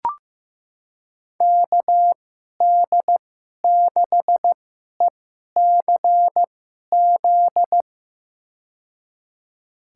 Morse code, heard as a keyed tone, about 15 wpm, spaced as for 11 wpm, 700 Hz. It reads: KD6ECZ
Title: USA Call Signs - 6 Characters - 2x Character Spacing - 15wpm